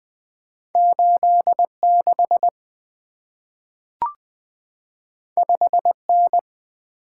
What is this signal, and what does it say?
Morse code, heard as a keyed tone, about 20 wpm, 700 Hz. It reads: 86 5N
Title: Two Letter-Number Combinations - 20wpm